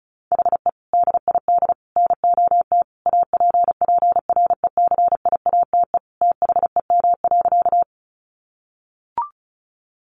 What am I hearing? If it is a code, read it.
Morse code, heard as a keyed tone, about 34 wpm, 700 Hz. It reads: HE DID NOT APPRECIATE THEM.
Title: Sentences from Top 1300 Words 35wpm